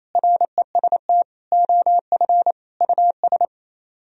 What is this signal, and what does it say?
Morse code, heard as a keyed tone, about 34 wpm, 700 Hz. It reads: REST OF US
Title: Speed-Racing - Sets of 4 Words 28wpm